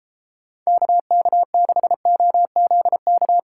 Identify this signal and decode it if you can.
Morse code, heard as a keyed tone, about 38 wpm, 700 Hz. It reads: KK6OZK